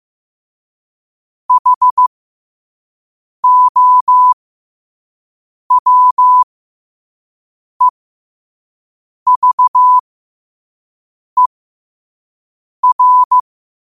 Morse code, heard as a keyed tone, about 15 wpm, spaced as for 5 wpm, 1000 Hz. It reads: HOWEVER
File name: qotdfiles / 5wpm